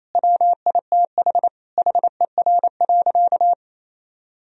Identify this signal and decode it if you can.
Morse code, heard as a keyed tone, about 28 wpm, 700 Hz. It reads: WITH HER.
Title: Sentences from Top 1000 Words - Review Part 1 - 28wpm